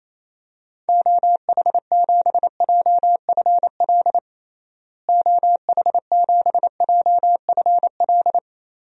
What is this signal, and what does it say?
Morse code, heard as a keyed tone, about 27 wpm, 700 Hz. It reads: OH7JFL OH7JFL
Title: DXCC-100 Call Signs - 6 Characters - Repeated 3x - 28wpm